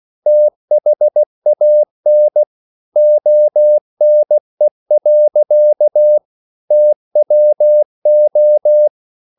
Morse code, heard as a keyed tone, about 16 wpm, 600 Hz. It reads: THAN ONE. TWO
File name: CW / Practice / EnglishSayings16